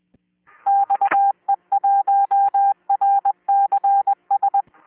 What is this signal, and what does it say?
Morse code, heard as a keyed tone, about 20 wpm, 800 Hz. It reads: XE1RCS